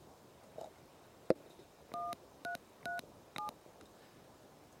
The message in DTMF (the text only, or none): 1337